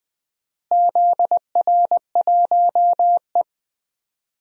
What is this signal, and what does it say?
Morse code, heard as a keyed tone, about 20 wpm, 700 Hz. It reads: ZR1E
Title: DXCC-100 Call Signs - 4 Characters - Repeated 2x - 20wpm